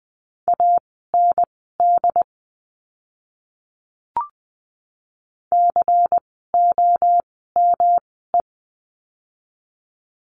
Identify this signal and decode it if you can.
Morse code, heard as a keyed tone, about 20 wpm, spaced as for 14 wpm, 700 Hz. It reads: AND COME
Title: 237 Lesson 38f - Letter q - Words 20wpm